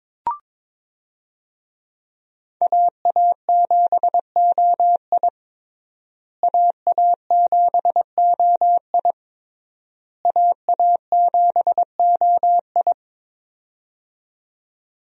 Morse code, heard as a keyed tone, about 22 wpm, 700 Hz. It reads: AA7OI AA7OI AA7OI
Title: USA Call Signs - 5 Characters - Repeated 3x - 22wpm